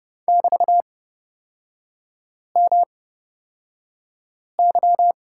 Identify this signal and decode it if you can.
Morse code, heard as a keyed tone, about 31 wpm, spaced as for 5 wpm, 700 Hz. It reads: <BT>MY
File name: Lesson 38e - Letter q - Three Characters 30wpm